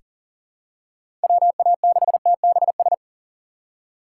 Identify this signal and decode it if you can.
Morse code, heard as a keyed tone, about 39 wpm, 700 Hz. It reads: WA6TBS
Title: USA Call Signs - 2x Character Spacing - 40wpm